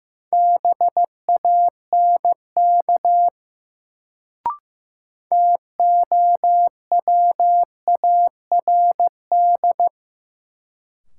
Morse code, heard as a keyed tone, about 15 wpm, 700 Hz. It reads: BANK TOWARD
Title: Sets of 5 Words - New Word Drill - 15wpm